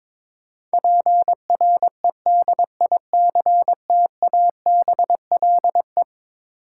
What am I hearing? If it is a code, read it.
Morse code, heard as a keyed tone, about 22 wpm, 700 Hz. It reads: PREDICTABLE
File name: practice / 22wpm / Eleven Letter Words - 22wpm